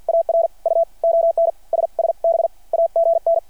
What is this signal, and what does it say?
Morse code, heard as a keyed tone, about 53 wpm, spaced as for 45 wpm, 660 Hz. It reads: AR U ON SSB AGN
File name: archived / CRSnet